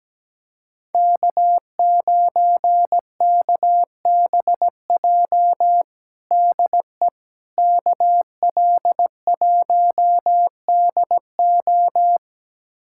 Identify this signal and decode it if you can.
Morse code, heard as a keyed tone, about 17 wpm, 700 Hz. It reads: K9KBJ DE KL1DO
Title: USA Callsign DE Callsign - 5 Characters - 17wpm